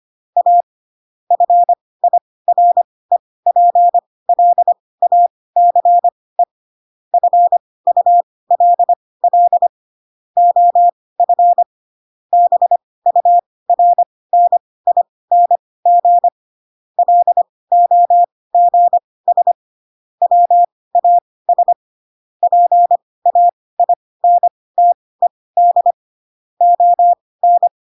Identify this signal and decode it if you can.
Morse code, heard as a keyed tone, about 25 wpm, spaced as for 18 wpm, 700 Hz. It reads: A FIREPLACE FULL OF BURNING LOGS WAS PAINTED ON
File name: Pinocchio ch3 12 25 ik1hge